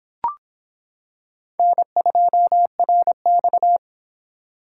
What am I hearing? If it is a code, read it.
Morse code, heard as a keyed tone, about 26 wpm, 700 Hz. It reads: N2RX